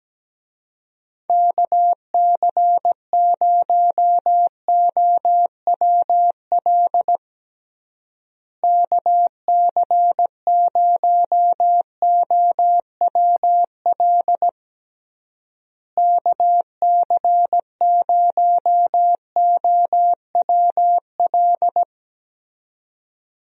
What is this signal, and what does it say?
Morse code, heard as a keyed tone, about 17 wpm, 700 Hz. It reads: KC0OWL KC0OWL KC0OWL
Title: USA Call Signs - 6 Characters - Repeated 3x - 17wpm